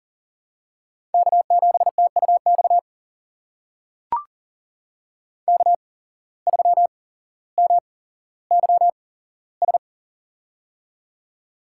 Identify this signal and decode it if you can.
Morse code, heard as a keyed tone, about 40 wpm, spaced as for 17 wpm, 700 Hz. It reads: K7TUX K3MYS
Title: USA Call Signs - 5 Characters - 8x Character Spacing - 40wpm